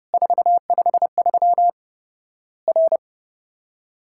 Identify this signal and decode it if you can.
Morse code, heard as a keyed tone, about 30 wpm, 700 Hz. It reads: 453 R
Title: Worked All Europe DX Contest - QTC Exchange - 30wpm